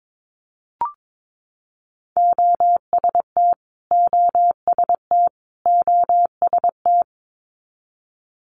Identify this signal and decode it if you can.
Morse code, heard as a keyed tone, about 22 wpm, 700 Hz. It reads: OST OST OST
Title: Multi-letter Phonemes Repeated 3x 22wpm